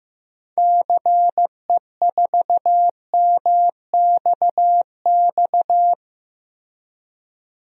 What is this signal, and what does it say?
Morse code, heard as a keyed tone, about 16 wpm, 700 Hz. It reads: CE4MXX
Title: DXCC-100 Call Signs - 6 Characters - Repeated 2x - 15wpm